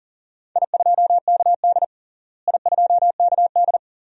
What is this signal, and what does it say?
Morse code, heard as a keyed tone, about 37 wpm, 700 Hz. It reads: I2KD I2KD
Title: DXCC-100 Call Signs - 4 Characters - Repeated 2x - 40wpm